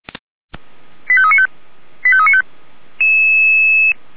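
Five-tone selective call (ZVEI), heard as quick runs of five tones, tones of about 70 ms. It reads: 85395 85395
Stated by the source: ZVEI five-tone call received on SDR.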